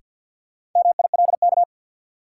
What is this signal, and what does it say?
Morse code, heard as a keyed tone, about 51 wpm, 700 Hz. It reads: MILK
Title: Sentences from Top 1500 Words - New Word Drill Alt - 50wpm